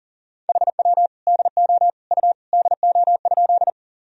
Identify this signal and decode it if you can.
Morse code, heard as a keyed tone, about 40 wpm, 700 Hz. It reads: HW DO U DO?
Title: Sentences from Top 100 Words and Abbreviations 40wpm